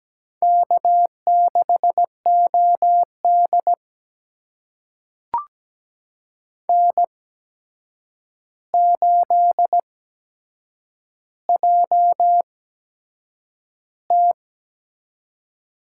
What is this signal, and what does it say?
Morse code, heard as a keyed tone, about 17 wpm, spaced as for 7 wpm, 700 Hz. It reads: K6OD N8JT
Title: USA Call Signs - 4 Characters - 8x Character Spacing - 17wpm